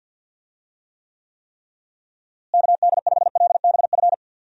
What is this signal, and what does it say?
Morse code, heard as a keyed tone, about 51 wpm, 700 Hz. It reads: KD5LBF
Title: USA Call Signs - 6 Characters - Repeated 3x - 50wpm